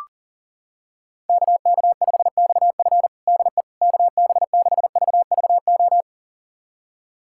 Morse code, heard as a keyed tone, about 37 wpm, 700 Hz. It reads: KK5XF DE KB6VVO
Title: USA Callsign DE Callsign 40wpm